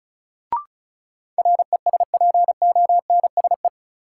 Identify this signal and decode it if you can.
Morse code, heard as a keyed tone, about 34 wpm, 700 Hz. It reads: RESPONSE
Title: Sentences from Top 1400 Words - New Word Drill - 35wpm